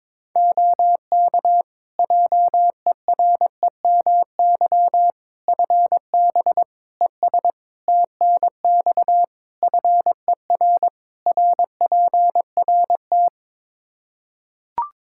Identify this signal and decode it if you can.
Morse code, heard as a keyed tone, about 22 wpm, 700 Hz. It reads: OK JEREMY FB ES TNX FER RPRT
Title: Practice QSO - Exchange 2 - Info - 22wpm